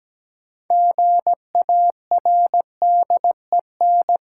Morse code, heard as a keyed tone, about 17 wpm, 700 Hz. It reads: GARDEN
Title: Sentences from Top 700 Words - New Word Drill - 17wpm